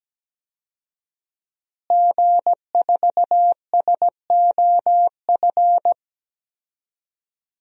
Morse code, heard as a keyed tone, about 17 wpm, 700 Hz. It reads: G4SOF
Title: DXCC-100 Call Signs - 5 Characters - Repeated 2x - 17wpm